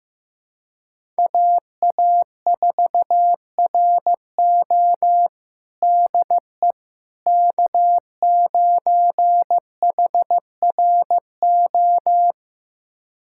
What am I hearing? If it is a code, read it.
Morse code, heard as a keyed tone, about 15 wpm, 700 Hz. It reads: AA4RO DE K9HRO
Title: USA Callsign DE Callsign - 5 Characters - 15wpm